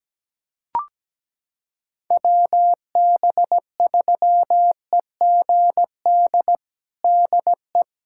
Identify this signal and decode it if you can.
Morse code, heard as a keyed tone, about 17 wpm, 700 Hz. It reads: WB3EGD DE
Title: USA Callsign DE Callsign - 6 Characters - 17wpm